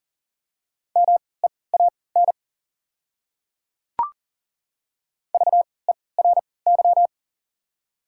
Morse code, heard as a keyed tone, about 40 wpm, spaced as for 23 wpm, 690 Hz. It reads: MEAN VERY
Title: 3x Character Spacing - Top 200 Words - 40wpm